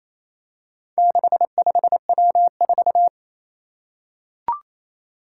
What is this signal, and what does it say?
Morse code, heard as a keyed tone, about 28 wpm, 700 Hz. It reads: 65W4
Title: Four Challenging Letter-Number Combinations 28wpm